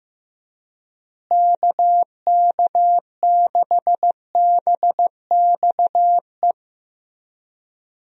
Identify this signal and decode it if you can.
Morse code, heard as a keyed tone, about 15 wpm, 700 Hz. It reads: KK6BXE
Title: USA Call Signs - 6 Characters - Repeated 3x - 15wpm